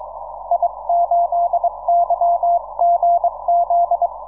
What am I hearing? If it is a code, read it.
Morse code, heard as a keyed tone, about 22 wpm, 690 Hz. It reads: I8YGZ
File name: iw0ffk / 2006 07 16 ZA1 I2AZ IW0FFK